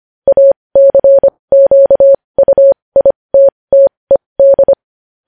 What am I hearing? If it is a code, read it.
Morse code, heard as a keyed tone, about 25 wpm, spaced as for 20 wpm, 550 Hz. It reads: ACQUITTED
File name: podcast / News headlines-1x-25x20